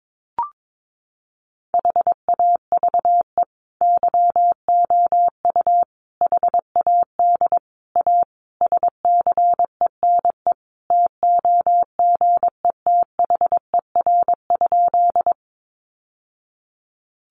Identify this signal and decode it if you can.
Morse code, heard as a keyed tone, about 22 wpm, 700 Hz. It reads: HAVE YOU HAD A SCENE TOGETHER?
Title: Sentences from Top 1100 Words 22wpm